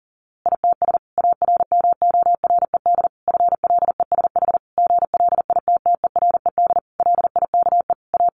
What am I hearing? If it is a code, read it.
Morse code, heard as a keyed tone, about 40 wpm, 700 Hz. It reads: ITS ARMORED FLESH GLITTERED LIKE A
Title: A Sound Of Thunder 40wpm